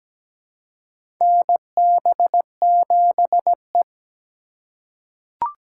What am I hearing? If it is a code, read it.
Morse code, heard as a keyed tone, about 17 wpm, 700 Hz. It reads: NB7E
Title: USA Call Signs - 4 Characters - 2x Character Spacing - 17wpm